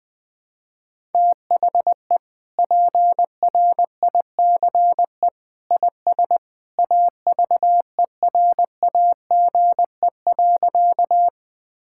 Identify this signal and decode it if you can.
Morse code, heard as a keyed tone, about 20 wpm, 700 Hz. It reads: THE PRICE IS AVERAGE.
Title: Sentences from Top 1000 Words - Review Part 6 - 20wpm